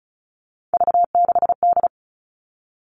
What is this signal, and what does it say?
Morse code, heard as a keyed tone, about 34 wpm, 700 Hz. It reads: V6D
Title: All Cut Numbers - 3-Digit 35wpm